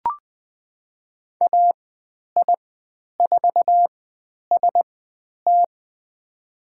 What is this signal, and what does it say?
Morse code, heard as a keyed tone, about 20 wpm, spaced as for 10 wpm, 700 Hz. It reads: AI4ST
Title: 102 Lesson 17g - Letter u - Call Signs 20wpm